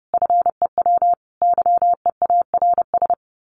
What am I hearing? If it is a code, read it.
Morse code, heard as a keyed tone, about 30 wpm, 700 Hz. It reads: FEW YEARS
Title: Sets of 5 Words Encore 30wpm